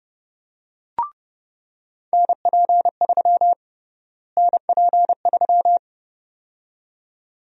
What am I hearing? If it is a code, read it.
Morse code, heard as a keyed tone, about 30 wpm, 700 Hz. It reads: NP3 NP3
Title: USA Call Sign Prefixes - Repeated 2x - 30wpm